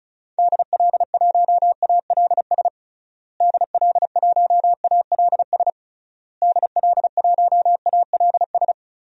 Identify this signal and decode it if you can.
Morse code, heard as a keyed tone, about 36 wpm, 700 Hz. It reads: DL1ALS DL1ALS DL1ALS
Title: DXCC-100 Call Signs - 6 Characters - Repeated 3x - 35wpm